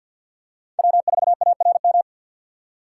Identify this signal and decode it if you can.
Morse code, heard as a keyed tone, about 51 wpm, 700 Hz. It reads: W4ARM
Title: USA Call Signs - 5 Characters - 6x Character Spacing - 50wpm